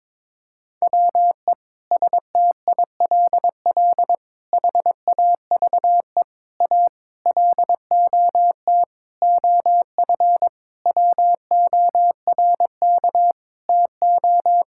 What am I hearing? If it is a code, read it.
Morse code, heard as a keyed tone, about 22 wpm, 700 Hz. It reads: WE STILL HAVE A LOT OF WORK TO
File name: Sets of 9 Words - 22wpm